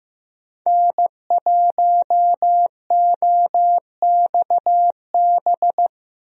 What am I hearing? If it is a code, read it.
Morse code, heard as a keyed tone, about 15 wpm, 700 Hz. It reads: N1OXB
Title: USA Call Signs - 5 Characters - 15wpm